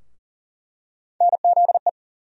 Morse code, heard as a keyed tone, about 40 wpm, 700 Hz. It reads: NZE
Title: USA Call Sign Postfixes - 3 Letters 40wpm